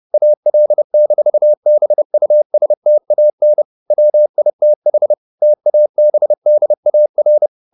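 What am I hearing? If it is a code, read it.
Morse code, heard as a keyed tone, about 30 wpm, 600 Hz. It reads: AL-BUSTAN WITH TABDAR
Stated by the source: Morse code at 30 wpm, the novel Never by Ken Follett in English, from CW med Gnister.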